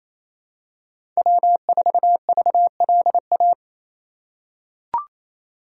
Morse code, heard as a keyed tone, about 28 wpm, 700 Hz. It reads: W4VLA
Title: USA Call Signs - 5 Characters - 4x Character Spacing - 28wpm